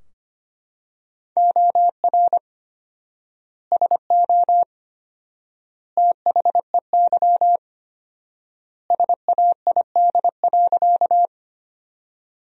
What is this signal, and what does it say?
Morse code, heard as a keyed tone, about 25 wpm, 700 Hz. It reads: OR SO THEY SAID.
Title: 4x Word-Spacing - Sentences from Top 100 Words - 25wpm